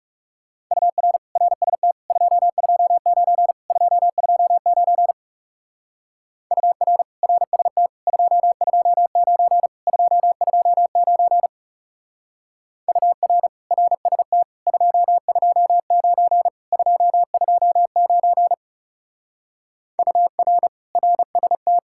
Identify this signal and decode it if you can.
Morse code, heard as a keyed tone, about 37 wpm, 700 Hz. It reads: UR RST 229 229 UR RST 229 229 UR RST 229 229 UR RST
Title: Speed-Racing Common QSO Phrases - 30wpm